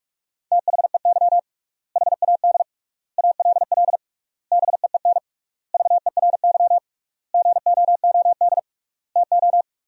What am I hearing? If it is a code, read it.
Morse code, heard as a keyed tone, about 47 wpm, 700 Hz. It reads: THEY HAD ALL BEEN VERY GOOD TO
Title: 3x Word-Spacing - Sentences from Top 200 Words - 45wpm